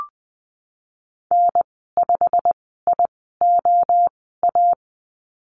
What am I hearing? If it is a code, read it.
Morse code, heard as a keyed tone, about 20 wpm, spaced as for 14 wpm, 700 Hz. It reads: N5IOA
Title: 173 Lesson 28g - Number 9 - Call Signs 20wpm